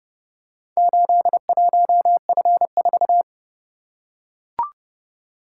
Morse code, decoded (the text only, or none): 81F4